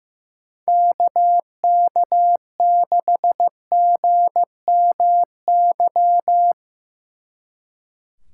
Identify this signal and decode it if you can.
Morse code, heard as a keyed tone, about 17 wpm, 700 Hz. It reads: KK6GMY
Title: Speed Racing - USA 2x3 Call Signs 15wpm